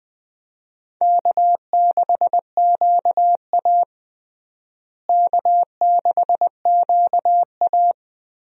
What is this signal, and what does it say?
Morse code, heard as a keyed tone, about 20 wpm, 700 Hz. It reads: K6QA K6QA
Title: USA Call Signs - 4 Characters - Repeated 2x - 20wpm